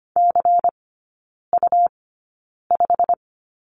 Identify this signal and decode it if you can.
Morse code, heard as a keyed tone, about 25 wpm, spaced as for 9 wpm, 700 Hz. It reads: CU5
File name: Lesson 17e - Letter u - Three Characters 25wpm